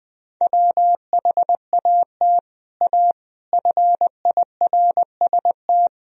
Morse code, heard as a keyed tone, about 21 wpm, 700 Hz. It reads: WHAT A FIRST
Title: Sentences from Top 1000 Words - Review Part 2 - 20wpm